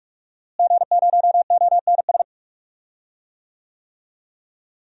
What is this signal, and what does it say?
Morse code, heard as a keyed tone, about 46 wpm, 700 Hz. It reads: G0ONS
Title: DXCC-100 Call Signs - 5 Characters - Repeated 3x - 45wpm